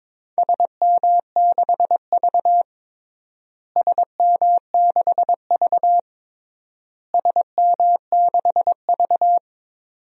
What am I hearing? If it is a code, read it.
Morse code, heard as a keyed tone, about 22 wpm, 700 Hz. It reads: SM6V SM6V SM6V
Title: DXCC-100 Call Signs - 4 Characters - Repeated 3x - 22wpm